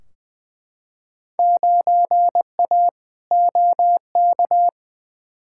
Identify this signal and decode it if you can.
Morse code, heard as a keyed tone, about 20 wpm, 700 Hz. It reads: 9A OK
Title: Field Day Exchanges Encore - 20wpm